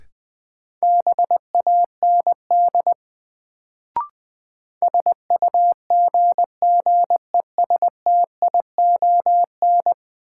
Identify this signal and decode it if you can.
Morse code, heard as a keyed tone, about 21 wpm, 700 Hz. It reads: BAND SUGGESTION
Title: Sentences from Top 1600 Words - New Word Drill - 20wpm